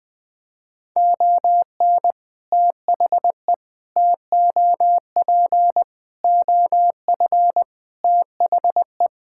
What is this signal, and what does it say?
Morse code, heard as a keyed tone, about 20 wpm, 700 Hz. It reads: ON THE TOP OF THE
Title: Sentences from Top 1000 Words - Review Part 6 - 20wpm